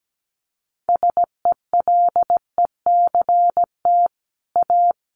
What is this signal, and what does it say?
Morse code, heard as a keyed tone, about 17 wpm, 700 Hz. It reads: SELECT A
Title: Sentences from Top 1300 Words 17wpm